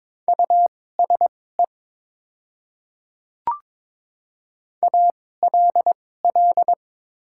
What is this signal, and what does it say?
Morse code, heard as a keyed tone, about 22 wpm, spaced as for 16 wpm, 690 Hz. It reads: USE ALL
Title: 2x Character Spacing - Top 200 Words - 22wpm